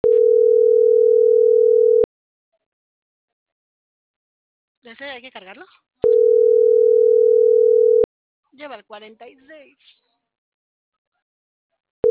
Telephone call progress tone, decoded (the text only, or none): ringback tone